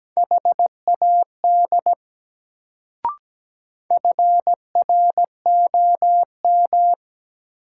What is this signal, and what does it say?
Morse code, heard as a keyed tone, about 17 wpm, 700 Hz. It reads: HAD FROM
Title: Sentences from Top 100 Words - New Word Drill - 17wpm